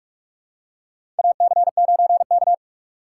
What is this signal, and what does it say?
Morse code, heard as a keyed tone, about 46 wpm, 700 Hz. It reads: AC9K
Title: USA Call Signs - 4x Character Spacing - 45wpm